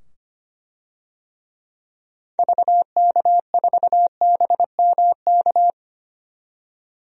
Morse code, heard as a keyed tone, about 25 wpm, 700 Hz. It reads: VK4BMK
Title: DXCC-100 Call Signs - 6 Characters - 25wpm